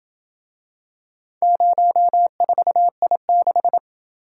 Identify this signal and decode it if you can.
Morse code, heard as a keyed tone, about 28 wpm, 700 Hz. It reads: 04I6